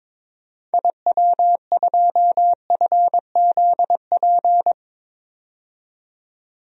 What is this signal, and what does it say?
Morse code, heard as a keyed tone, about 22 wpm, 700 Hz. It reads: IW2FZP
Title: DXCC-100 Call Signs - 6 Characters - 22wpm